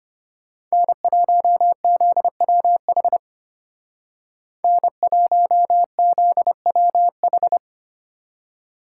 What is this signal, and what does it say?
Morse code, heard as a keyed tone, about 33 wpm, 700 Hz. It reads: N1ZWH N1ZWH